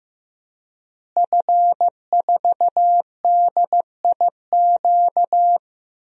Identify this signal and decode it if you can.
Morse code, heard as a keyed tone, about 15 wpm, 700 Hz. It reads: F4DIQ
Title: DXCC-100 Call Signs - 5 Characters - Repeated 3x - 15wpm